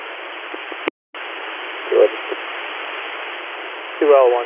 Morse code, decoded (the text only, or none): none